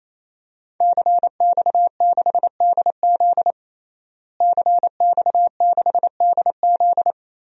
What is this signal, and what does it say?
Morse code, heard as a keyed tone, about 28 wpm, 700 Hz. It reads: CX6DZ CX6DZ
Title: DXCC-100 Call Signs - 5 Characters - Repeated 3x - 28wpm